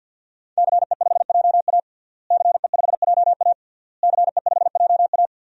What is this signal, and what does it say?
Morse code, heard as a keyed tone, about 52 wpm, 700 Hz. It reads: CE5JA CE5JA CE5JA
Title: DXCC-100 Call Signs - 5 Characters - Repeated 3x - 50wpm